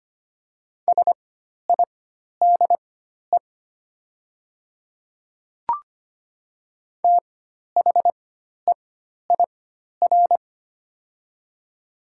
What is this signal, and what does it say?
Morse code, heard as a keyed tone, about 25 wpm, spaced as for 12 wpm, 700 Hz. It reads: SIDE THEIR
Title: 4x Character Spacing - Top 100 Words - 25wpm